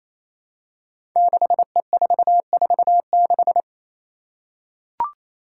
Morse code, heard as a keyed tone, about 28 wpm, 700 Hz. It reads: 6E446
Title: Five Challenging Letter-Number Combinations 28wpm